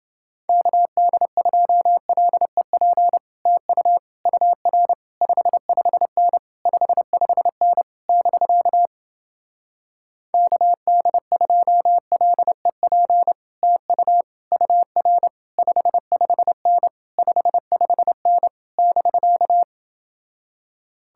Morse code, heard as a keyed tone, about 29 wpm, 700 Hz. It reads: KD2LEP TU UR 55N 55N <BK> KD2LEP TU UR 55N 55N <BK>